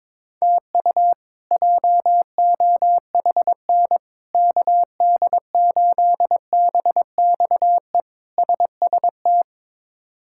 Morse code, heard as a keyed tone, about 22 wpm, 700 Hz. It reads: TU JOHN KD8BXE SST
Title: K1USN SST QSO Practice - 22wpm